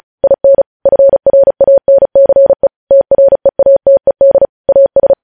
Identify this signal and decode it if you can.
Morse code, heard as a keyed tone, about 36 wpm, 550 Hz. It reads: IN FRANCE TREATED AS
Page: news headlines in morse code at 35 wpm